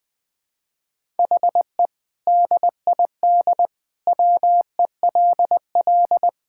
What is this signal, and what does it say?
Morse code, heard as a keyed tone, about 20 wpm, 700 Hz. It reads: HE DID WELL
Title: Sentences with 2-Word Phrases - 20wpm